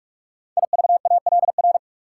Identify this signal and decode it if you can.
Morse code, heard as a keyed tone, about 44 wpm, 690 Hz. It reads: IVALR
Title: Code Groups - 5 Letters - 45wpm